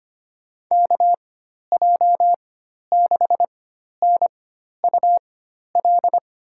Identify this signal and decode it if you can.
Morse code, heard as a keyed tone, about 25 wpm, spaced as for 12 wpm, 700 Hz. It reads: KJ6NUL
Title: USA Call Signs - 6 Characters - 4x Character Spacing - 25wpm